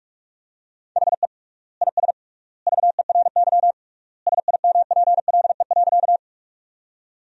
Morse code, heard as a keyed tone, about 41 wpm, 700 Hz. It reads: HE IS VERY SIMPLE.
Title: 3x Word-Spacing - Sentences from Top 400 Words - 45wpm